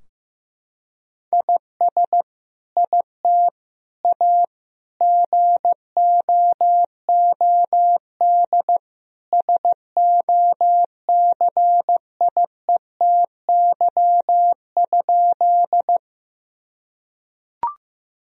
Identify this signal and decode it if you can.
Morse code, heard as a keyed tone, about 15 wpm, 700 Hz. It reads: IS IT A GOOD SOCIETY?
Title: Sentences from Top 1000 Words - Review Part 4 - 15wpm